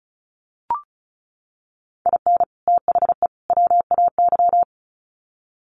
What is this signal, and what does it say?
Morse code, heard as a keyed tone, about 34 wpm, 700 Hz. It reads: IN THE WAY
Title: Sets of 3 Words 35wpm